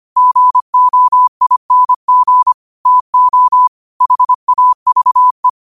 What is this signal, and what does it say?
Morse code, heard as a keyed tone, about 25 wpm, 1000 Hz. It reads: GOING TO HAVE